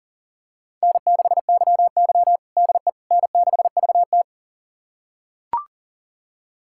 Morse code, heard as a keyed tone, about 40 wpm, 700 Hz. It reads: N6YY DE N6VT
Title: USA Callsign DE Callsign - 4 Characters - 40wpm